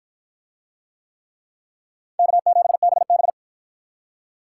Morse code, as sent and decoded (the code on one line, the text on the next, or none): -.- --... -... -...
K7BB